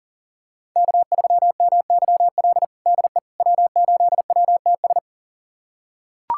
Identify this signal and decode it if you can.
Morse code, heard as a keyed tone, about 40 wpm, 700 Hz. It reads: K3MYL DE W8WTS
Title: USA Callsign DE Callsign - 5 Characters - 40wpm